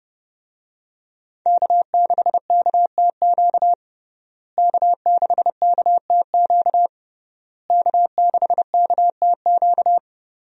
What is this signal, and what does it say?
Morse code, heard as a keyed tone, about 30 wpm, 700 Hz. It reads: K6KTQ K6KTQ K6KTQ
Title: USA Call Signs - 5 Characters - Repeated 3x - 30wpm